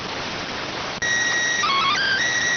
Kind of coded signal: SSTV (slow-scan television)

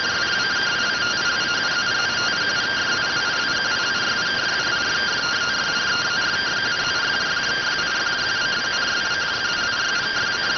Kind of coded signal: two-tone teleprinter signal

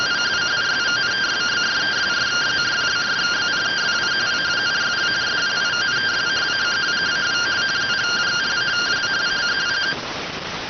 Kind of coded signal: two-tone teleprinter signal

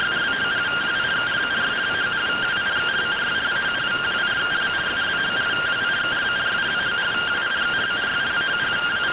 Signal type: two-tone teleprinter signal

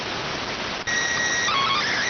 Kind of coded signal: SSTV (slow-scan television)

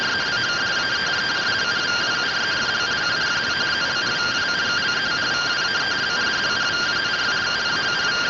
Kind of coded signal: two-tone teleprinter signal

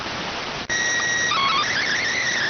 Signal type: SSTV (slow-scan television)

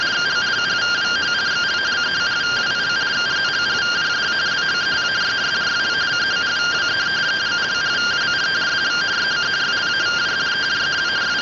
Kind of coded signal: two-tone teleprinter signal

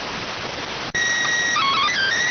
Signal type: SSTV (slow-scan television)

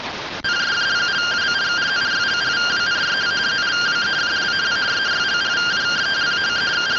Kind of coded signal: two-tone teleprinter signal

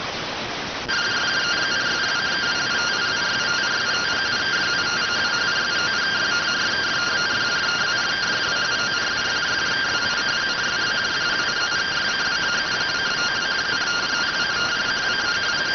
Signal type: two-tone teleprinter signal

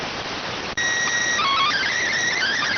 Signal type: SSTV (slow-scan television)

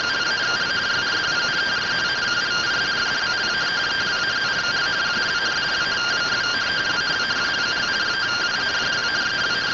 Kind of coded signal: two-tone teleprinter signal

